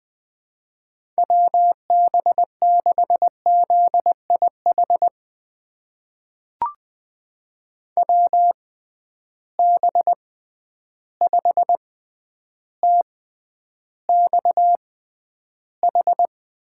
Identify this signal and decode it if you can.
Morse code, heard as a keyed tone, about 20 wpm, spaced as for 10 wpm, 700 Hz. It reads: WB6ZIH WB5TXH
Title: USA Call Signs - 6 Characters - 6x Character Spacing - 20wpm